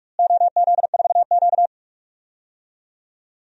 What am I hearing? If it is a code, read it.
Morse code, heard as a keyed tone, about 42 wpm, 700 Hz. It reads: OZ4Q
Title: DXCC-100 Call Signs - 4 Characters - Repeated 3x - 45wpm